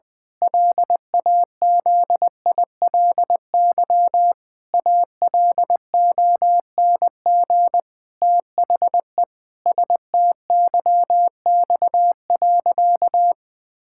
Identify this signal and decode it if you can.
Morse code, heard as a keyed tone, about 20 wpm, 700 Hz. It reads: LAZILY ALONG THE STYX.
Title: Pangram Challenge v2 - 20wpm